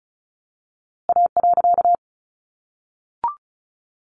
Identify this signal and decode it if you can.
Morse code, heard as a keyed tone, about 34 wpm, 700 Hz. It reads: A.
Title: A Sound Of Thunder 35wpm